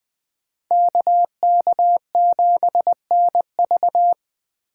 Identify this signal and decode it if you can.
Morse code, heard as a keyed tone, about 21 wpm, 700 Hz. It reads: KK7NV